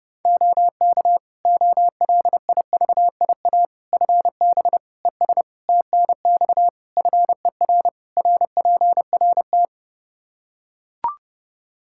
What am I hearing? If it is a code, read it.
Morse code, heard as a keyed tone, about 30 wpm, 700 Hz. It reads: OK OLIVIA FB ES TNX FER RPRT
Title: Practice QSO - Exchange 2 - Info - 30wpm